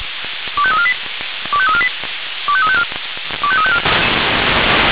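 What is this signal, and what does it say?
Five-tone selective call (ZVEI), heard as quick runs of five tones, tones of about 70 ms. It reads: 25348 25348 25354 25354
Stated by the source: ZVEI five-tone selective call received by SDR.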